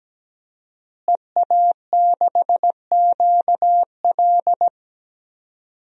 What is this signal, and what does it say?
Morse code, heard as a keyed tone, about 17 wpm, 700 Hz. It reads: EA6QL
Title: DXCC-100 Call Signs - 5 Characters - Repeated 2x - 17wpm